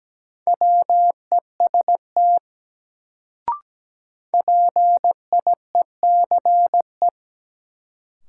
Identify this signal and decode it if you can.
Morse code, heard as a keyed tone, about 17 wpm, 700 Hz. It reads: WEST PIECE
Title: Sentences from Top 400 Words - New Word Drill - 17wpm